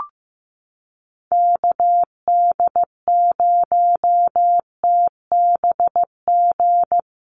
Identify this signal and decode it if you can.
Morse code, heard as a keyed tone, about 15 wpm, 700 Hz. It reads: KD0TBG
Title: USA Call Signs Repeated 3x 15wpm